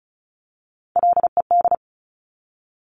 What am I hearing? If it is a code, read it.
Morse code, heard as a keyed tone, about 36 wpm, 700 Hz. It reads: LED
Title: Top Three Letter Combinations in Words 35wpm